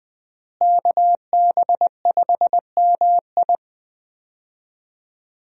Morse code, heard as a keyed tone, about 21 wpm, 700 Hz. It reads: KB5MI
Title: USA Call Signs - 5 Characters - Repeated 3x - 20wpm